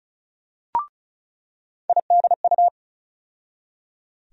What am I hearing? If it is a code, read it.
Morse code, heard as a keyed tone, about 36 wpm, 700 Hz. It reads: IDU
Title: USA Call Sign Postfixes - 3 Letters 35wpm